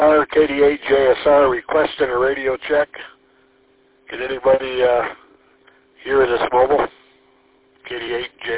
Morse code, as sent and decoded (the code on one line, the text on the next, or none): none